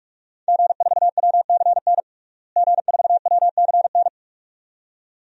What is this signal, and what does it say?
Morse code, heard as a keyed tone, about 46 wpm, 700 Hz. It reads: G4WCN G4WCN
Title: DXCC-100 Call Signs - 5 Characters - Repeated 2x - 45wpm